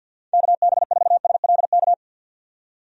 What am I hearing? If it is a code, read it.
Morse code, heard as a keyed tone, about 50 wpm, 700 Hz. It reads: KB4SLK
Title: USA Call Signs - 6 Characters - 50wpm